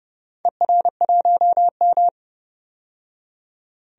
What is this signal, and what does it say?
Morse code, heard as a keyed tone, about 30 wpm, 700 Hz. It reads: ER1M
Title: DXCC-100 Call Signs - 4 Characters - 30wpm